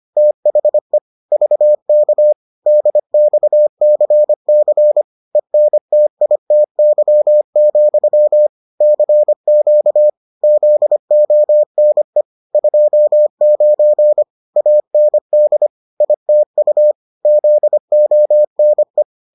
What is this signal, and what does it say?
Morse code, heard as a keyed tone, about 25 wpm, 600 Hz. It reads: THE VK DXCC ENTITY, CQ ZONE 29 AND ITU ZONE